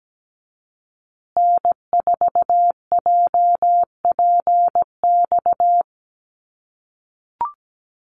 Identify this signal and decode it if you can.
Morse code, heard as a keyed tone, about 17 wpm, 700 Hz. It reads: N4JPX
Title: USA Call Signs Repeated 3x 17wpm